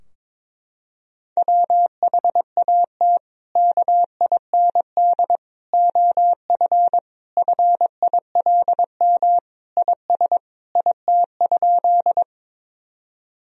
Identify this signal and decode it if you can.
Morse code, heard as a keyed tone, about 22 wpm, 700 Hz. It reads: WHAT KIND OF FILM IS IT?
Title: Sentences from Top 1000 Words - Review Part 3 - 22wpm